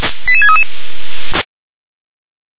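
Five-tone selective call (ZVEI), heard as quick runs of five tones, tones of about 70 ms. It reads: 88633
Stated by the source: ZVEI five-tone selective call received on SDR.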